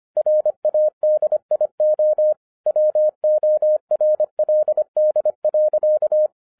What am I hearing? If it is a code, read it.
Morse code, heard as a keyed tone, about 25 wpm, 610 Hz. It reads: RADIO WORLD.